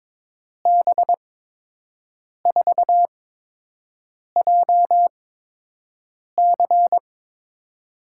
Morse code, heard as a keyed tone, about 22 wpm, spaced as for 6 wpm, 700 Hz. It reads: B4JC
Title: USA Call Signs - 5 Characters - 8x Character Spacing - 22wpm